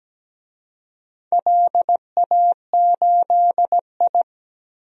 Morse code, heard as a keyed tone, about 17 wpm, 700 Hz. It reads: LA8I
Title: DXCC-100 Call Signs - 4 Characters - Repeated 3x - 17wpm